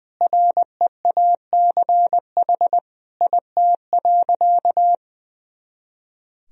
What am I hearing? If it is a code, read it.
Morse code, heard as a keyed tone, about 20 wpm, 700 Hz. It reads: REACH IT.
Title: Sentences from Top 400 Words and Abbreviations 20wpm